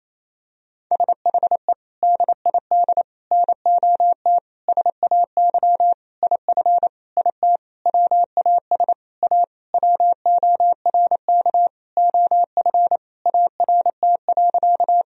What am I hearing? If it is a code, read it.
Morse code, heard as a keyed tone, about 28 wpm, 700 Hz. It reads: SHE DID NOT SAY IF IT WAS A WORK OF ART.
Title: Sentences from Top 1000 Words - Review Part 3 - 28wpm